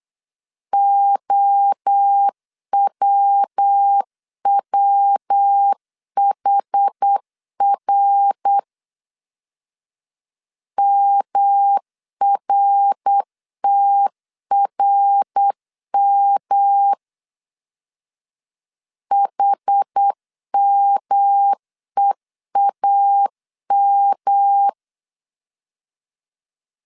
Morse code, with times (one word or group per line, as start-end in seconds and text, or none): 0.7-8.6 OWWHR
10.8-16.9 MRTRM
19.1-24.7 HMEAM